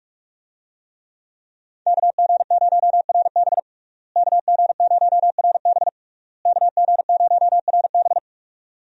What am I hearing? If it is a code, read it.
Morse code, heard as a keyed tone, about 46 wpm, 700 Hz. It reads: KG0RB KG0RB KG0RB
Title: USA Call Signs - 5 Characters - Repeated 3x - 45wpm